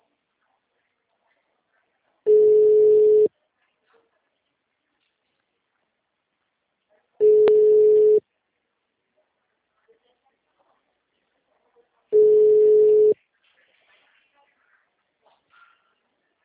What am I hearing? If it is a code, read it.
Ringback tone (Europe, 425 Hz).